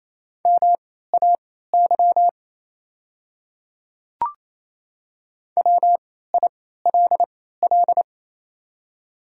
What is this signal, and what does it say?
Morse code, heard as a keyed tone, about 27 wpm, spaced as for 16 wpm, 690 Hz. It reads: MAY WILL
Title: 3x Character Spacing - Top 200 Words - 28wpm